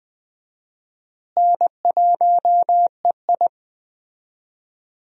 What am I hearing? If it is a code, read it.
Morse code, heard as a keyed tone, about 20 wpm, 700 Hz. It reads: N1EI
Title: USA Call Signs - 4 Characters - Repeated 2x - 20wpm